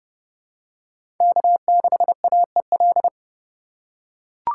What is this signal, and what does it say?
Morse code, heard as a keyed tone, about 29 wpm, 700 Hz. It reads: K6AEL
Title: USA Call Signs - 5 Characters - 3x Character Spacing - 30wpm